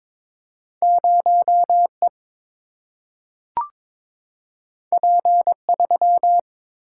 Morse code, heard as a keyed tone, about 22 wpm, 700 Hz. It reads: 0 E P 3
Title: Single-Letter Single-Number 22wpm